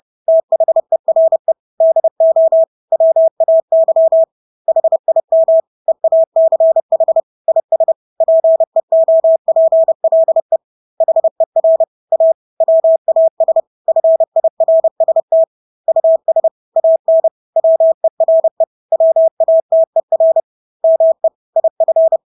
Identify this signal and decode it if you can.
Morse code, heard as a keyed tone, about 30 wpm, 650 Hz. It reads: THERE DO WAY HIM EACH IS PEOPLE HER A WAS FIRST US AN WERE WATER ME IF
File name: Engelske ord 30-30 wpm